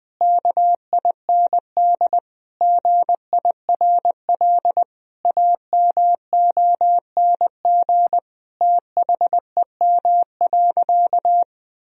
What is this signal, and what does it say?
Morse code, heard as a keyed tone, about 21 wpm, 700 Hz. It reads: KIND GIRL AMONG THEM.
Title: Sentences from Top 1000 Words - Review Part 3 - 20wpm